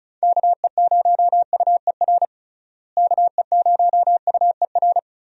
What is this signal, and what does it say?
Morse code, heard as a keyed tone, about 34 wpm, 700 Hz. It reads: KE0UER KE0UER
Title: USA Call Signs - 6 Characters - Repeated 3x - 35wpm